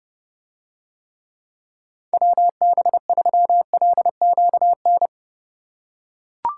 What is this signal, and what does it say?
Morse code, heard as a keyed tone, about 30 wpm, 700 Hz. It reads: WB3LQN